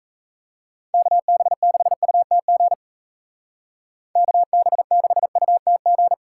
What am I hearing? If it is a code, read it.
Morse code, heard as a keyed tone, about 36 wpm, 700 Hz. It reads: KB6UTG KB6UTG